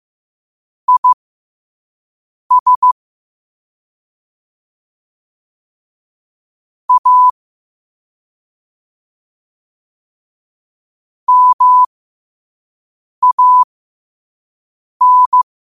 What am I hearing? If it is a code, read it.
Morse code, heard as a keyed tone, about 15 wpm, spaced as for 5 wpm, 1000 Hz. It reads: IS A MAN